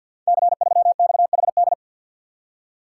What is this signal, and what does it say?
Morse code, heard as a keyed tone, about 52 wpm, 700 Hz. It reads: C3XHD
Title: Code Groups - 5 Letter:Numbers - 50wpm